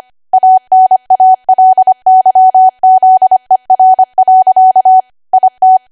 Morse code, heard as a keyed tone, about 25 wpm, 750 Hz. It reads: ANALYZER. IT
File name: Archive / 25 WPM / 25WPM